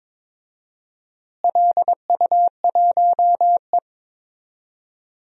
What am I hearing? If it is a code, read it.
Morse code, heard as a keyed tone, about 22 wpm, 700 Hz. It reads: LU1E
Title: DXCC-100 Call Signs - 4 Characters - Repeated 3x - 22wpm